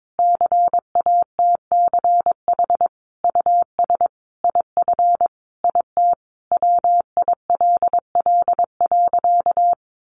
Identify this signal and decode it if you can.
Morse code, heard as a keyed tone, about 22 wpm, 700 Hz. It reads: CATCH US IF IT WILL.
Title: Sentences from Top 1500 Words 22wpm